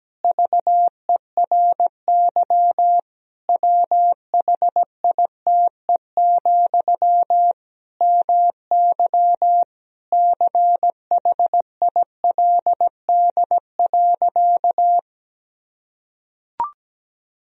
Morse code, heard as a keyed tone, about 17 wpm, 700 Hz. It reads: VERY WHITE, MY CHILD.
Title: Sentences from Top 1000 Words - Review Part 1 - 17wpm